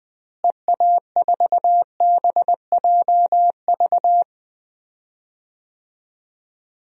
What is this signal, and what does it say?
Morse code, heard as a keyed tone, about 20 wpm, 700 Hz. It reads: EA4BJV
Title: DXCC-100 Call Signs - 6 Characters - Repeated 3x - 20wpm